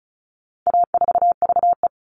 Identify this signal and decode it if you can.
Morse code, heard as a keyed tone, about 34 wpm, 700 Hz. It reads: A4VE
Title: All Cut Numbers - 4-Digit 35wpm